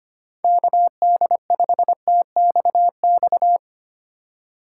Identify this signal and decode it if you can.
Morse code, heard as a keyed tone, about 25 wpm, 700 Hz. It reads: KD5TXX
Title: K1USN SST QSO Practice - 25wpm